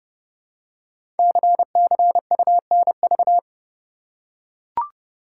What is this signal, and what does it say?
Morse code, heard as a keyed tone, about 30 wpm, 700 Hz. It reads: CCUNV
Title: Code Groups - 5 Letters - 30wpm